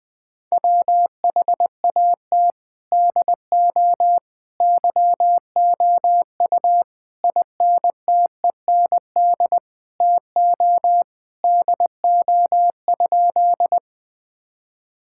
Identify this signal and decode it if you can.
Morse code, heard as a keyed tone, about 19 wpm, 700 Hz. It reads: WHAT DO YOU INTEND TO DO?
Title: Sentences from Top 1600 Words 20wpm